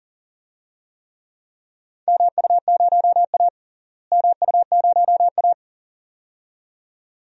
Morse code, heard as a keyed tone, about 39 wpm, 700 Hz. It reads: MU0A MU0A
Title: DXCC-100 Call Signs - 4 Characters - Repeated 2x - 40wpm